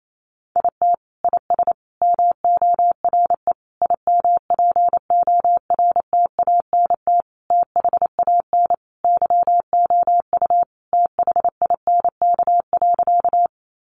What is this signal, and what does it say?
Morse code, heard as a keyed tone, about 28 wpm, 700 Hz. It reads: IT IS MORE IMPORTANT THAN YOU THINK.
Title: Sentences from Top 600 Words 28wpm